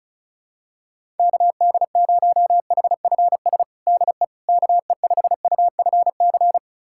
Morse code, heard as a keyed tone, about 36 wpm, 700 Hz. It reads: KD0HFS DE KE5UFC
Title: USA Callsign DE Callsign - 6 Characters - 35wpm